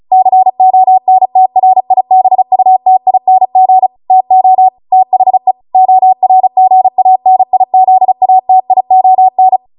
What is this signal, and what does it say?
Morse code, heard as a keyed tone, about 36 wpm, 750 Hz. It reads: CONTRIBUTING TO THE ORGANIZATION